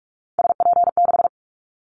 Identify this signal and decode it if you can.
Morse code, heard as a keyed tone, about 44 wpm, 700 Hz. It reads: SP6
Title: Callsign Prefixes for Top 25 DXCC - 45wpm